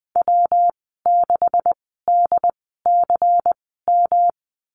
Morse code, heard as a keyed tone, about 19 wpm, spaced as for 14 wpm, 700 Hz. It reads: W6DCM
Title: 147 Lesson 24g - Letter y - Call Signs 20wpm